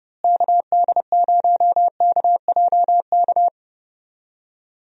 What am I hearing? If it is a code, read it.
Morse code, heard as a keyed tone, about 30 wpm, 700 Hz. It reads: KD0KJK
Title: USA Call Signs - 6 Characters - Repeated 3x - 30wpm